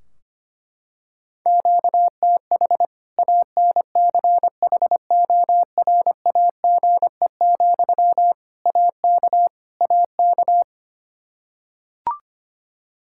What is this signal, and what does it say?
Morse code, heard as a keyed tone, about 25 wpm, 700 Hz. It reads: QTH ANCHORAGE, AK AK
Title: Practice QSO - Exchange 1 - Beginning - 25wpm